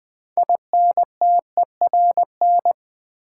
Morse code, heard as a keyed tone, about 20 wpm, 700 Hz. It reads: INTERN
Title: Build A Word - 2 Letters At A Time - 8-Letter Words - 20wpm